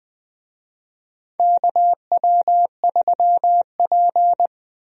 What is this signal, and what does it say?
Morse code, heard as a keyed tone, about 20 wpm, 700 Hz. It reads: KW3P
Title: USA Call Signs - 4 Characters - Repeated 3x - 20wpm